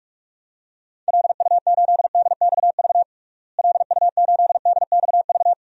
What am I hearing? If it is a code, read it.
Morse code, heard as a keyed tone, about 42 wpm, 700 Hz. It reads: LU8DXV LU8DXV
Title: DXCC-100 Call Signs - 6 Characters - Repeated 2x - 45wpm